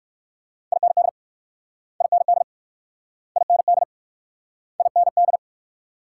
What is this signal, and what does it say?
Morse code, heard as a keyed tone, about 60 wpm, 700 Hz. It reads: IND IND IND IND